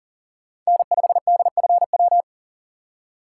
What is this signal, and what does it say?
Morse code, heard as a keyed tone, about 40 wpm, 700 Hz. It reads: N5DFW
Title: USA Call Signs - 5 Characters - 2x Character Spacing - 40wpm